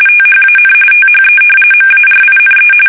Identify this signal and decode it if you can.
RTTY, about 45.45 baud. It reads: THAN THE MARINE
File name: fsk850